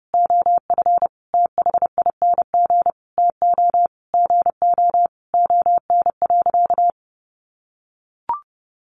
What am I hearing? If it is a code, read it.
Morse code, heard as a keyed tone, about 30 wpm, 700 Hz. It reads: OF THING TO GO ON.
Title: Sentences from Top 1300 Words 30wpm